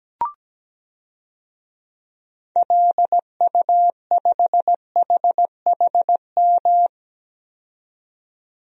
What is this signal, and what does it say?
Morse code, heard as a keyed tone, about 17 wpm, 700 Hz. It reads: LU5HHM
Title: DXCC-100 Call Signs - 6 Characters - 17wpm